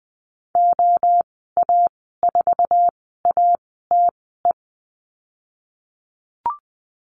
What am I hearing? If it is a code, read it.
Morse code, heard as a keyed tone, about 20 wpm, spaced as for 14 wpm, 700 Hz. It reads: OA4ATE
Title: 161 Lesson 26g - Letter g - Call Signs 20wpm